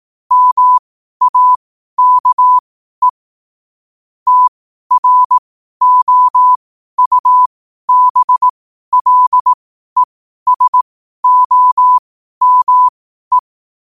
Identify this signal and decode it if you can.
Morse code, heard as a keyed tone, about 18 wpm, spaced as for 12 wpm, 1000 Hz. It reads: MAKE TROUBLESOME
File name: qotdfiles / 12wpm